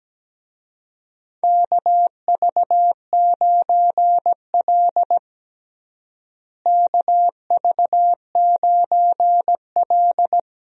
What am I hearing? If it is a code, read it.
Morse code, heard as a keyed tone, about 17 wpm, 700 Hz. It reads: KV9L KV9L
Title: USA Call Signs - 4 Characters - Repeated 2x - 17wpm